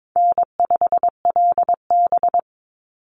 Morse code, heard as a keyed tone, about 22 wpm, 700 Hz. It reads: N5LB
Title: Easy Call Signs Repeated 3x - 22wpm